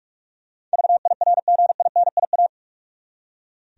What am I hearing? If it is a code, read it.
Morse code, heard as a keyed tone, about 46 wpm, 700 Hz. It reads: VIRGINIA
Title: USA Common Names - 45wpm